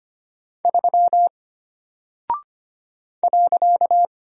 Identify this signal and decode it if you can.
Morse code, heard as a keyed tone, about 25 wpm, 700 Hz. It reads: 3 .
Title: Single Character (Letter:Number:Punctuation:Prosign) Alt 25wpm